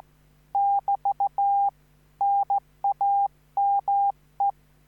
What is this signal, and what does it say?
Morse code, heard as a keyed tone, about 15 wpm, 800 Hz. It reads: <BT> NAME
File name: QSO03